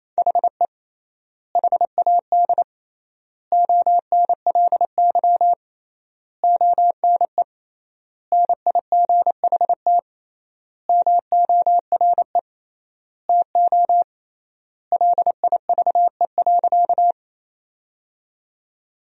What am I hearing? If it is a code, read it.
Morse code, heard as a keyed tone, about 28 wpm, 700 Hz. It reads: HE HAD ONLY ONE NIGHT MORE TO LIVE.
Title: 3x Word-Spacing - Sentences from Top 300 Words - 28wpm